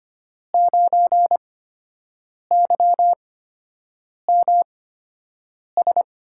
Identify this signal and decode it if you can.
Morse code, heard as a keyed tone, about 25 wpm, spaced as for 7 wpm, 700 Hz. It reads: 9YMS
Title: USA Call Signs - 8x Character Spacing - 25wpm